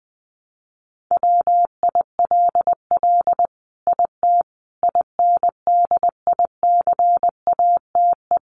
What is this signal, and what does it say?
Morse code, heard as a keyed tone, about 20 wpm, 700 Hz. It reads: WILL IT INDICATE
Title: Sentences from Top 1300 Words 20wpm